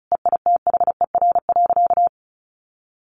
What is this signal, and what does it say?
Morse code, heard as a keyed tone, about 34 wpm, 700 Hz. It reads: EITHER.
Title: Sentences from Top 1000 Words 35wpm